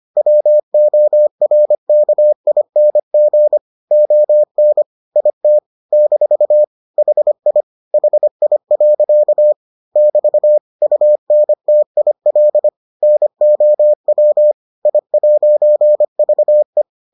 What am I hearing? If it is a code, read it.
Morse code, heard as a keyed tone, about 25 wpm, 600 Hz. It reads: WORKING ON IT - HI HI. <BT> UNTIL NOW I'VE